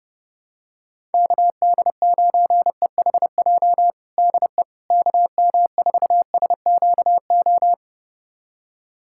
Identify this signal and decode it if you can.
Morse code, heard as a keyed tone, about 32 wpm, 700 Hz. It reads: KD9EHJ DE KM4SQO
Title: USA Callsign DE Callsign 30wpm